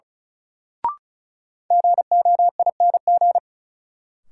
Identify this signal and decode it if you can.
Morse code, heard as a keyed tone, about 34 wpm, 700 Hz. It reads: GOING
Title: Sentences from Top 500 Words - New Word Drill - 35wpm